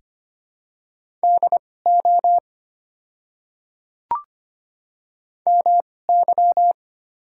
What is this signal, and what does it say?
Morse code, heard as a keyed tone, about 25 wpm, spaced as for 18 wpm, 700 Hz. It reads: DO MY
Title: 2x Character Spacing - Top 100 Words - 25wpm